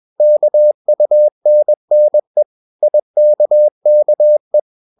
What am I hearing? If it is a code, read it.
Morse code, heard as a keyed tone, about 21 wpm, 600 Hz. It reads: KUNNE IKKE
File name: Ferie QRS 21wpm